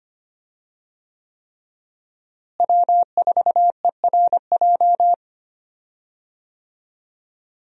Morse code, heard as a keyed tone, about 25 wpm, 700 Hz. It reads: W4ERJ
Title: USA Call Signs - 5 Characters - Repeated 3x - 25wpm